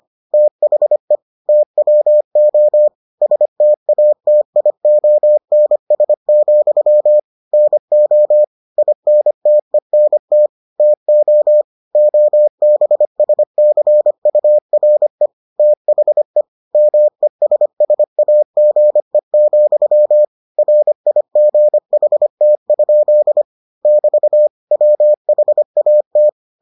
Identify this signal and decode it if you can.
Morse code, heard as a keyed tone, about 25 wpm, 600 Hz. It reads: THE TWO STATIONS, NO INTENT TO OBSCURE THE MESSAGE, RIGHT? <BT> WHAT